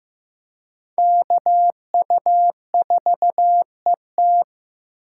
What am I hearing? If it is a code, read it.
Morse code, heard as a keyed tone, about 15 wpm, 700 Hz. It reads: KU4ET
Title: USA Call Signs - 4x Character Spacing - 15wpm